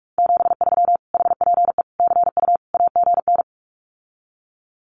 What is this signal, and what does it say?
Morse code, heard as a keyed tone, about 44 wpm, 700 Hz. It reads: 73 HPE CU AGN
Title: Common QSO Phrases Encore 45wpm